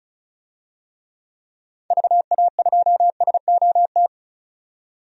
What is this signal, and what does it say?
Morse code, heard as a keyed tone, about 36 wpm, 690 Hz. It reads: VA2SOT
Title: DXCC-100 Call Signs - 6 Characters - 35wpm